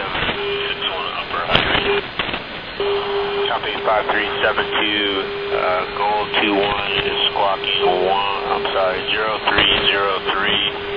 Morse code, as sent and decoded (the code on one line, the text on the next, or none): none